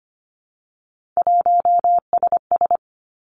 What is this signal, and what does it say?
Morse code, heard as a keyed tone, about 25 wpm, 700 Hz. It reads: 1SS
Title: Three Challenging Letter-Number Combinations 25wpm